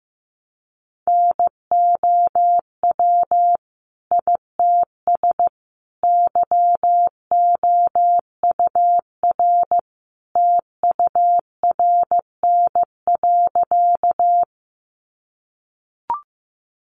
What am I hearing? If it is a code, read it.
Morse code, heard as a keyed tone, about 15 wpm, 700 Hz. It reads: NOW ITS YOUR TURN.
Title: Sentences from Top 600 Words 15wpm